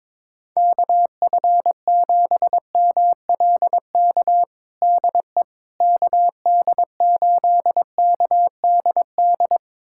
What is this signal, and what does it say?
Morse code, heard as a keyed tone, about 21 wpm, 700 Hz. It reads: KF7MLK DE KD8KDD
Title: USA Callsign DE Callsign 22wpm